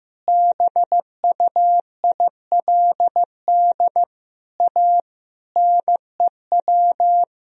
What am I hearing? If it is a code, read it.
Morse code, heard as a keyed tone, about 14 wpm, 700 Hz. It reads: BUILD A NEW
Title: Sentences from Top 1000 Words - Review Part 3 - 15wpm